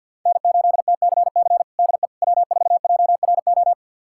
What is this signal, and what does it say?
Morse code, heard as a keyed tone, about 50 wpm, 700 Hz. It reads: N8TCC DE W4JRO
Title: USA Callsign DE Callsign - 5 Characters - 50wpm